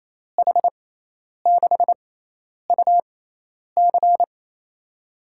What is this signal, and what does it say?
Morse code, heard as a keyed tone, about 28 wpm, spaced as for 10 wpm, 700 Hz. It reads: H6UC
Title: USA Call Signs - 5 Characters - 6x Character Spacing - 28wpm